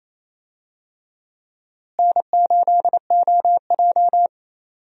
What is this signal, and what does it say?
Morse code, heard as a keyed tone, about 28 wpm, 700 Hz. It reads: N8OJ
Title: USA Call Signs - 4 Characters - Repeated 2x - 28wpm